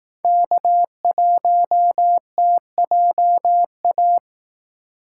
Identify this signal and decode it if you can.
Morse code, heard as a keyed tone, about 18 wpm, 700 Hz. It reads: K1TJA
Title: Speed Racing - USA 1x3 Call Signs 15wpm